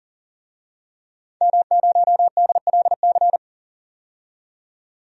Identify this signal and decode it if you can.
Morse code, heard as a keyed tone, about 40 wpm, 700 Hz. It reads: M0DLC
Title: DXCC-100 Call Signs - 5 Characters - 40wpm